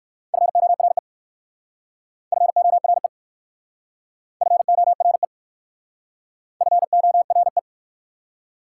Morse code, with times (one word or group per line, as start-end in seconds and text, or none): 0.3-1.0 FORE
2.3-3.1 FORE
4.4-5.2 FORE
6.6-7.6 FORE